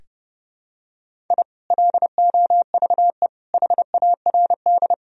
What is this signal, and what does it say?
Morse code, heard as a keyed tone, about 32 wpm, 700 Hz. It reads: I LOVE HARD
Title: Sentences from Top 1000 Words - Review Part 1 - 30wpm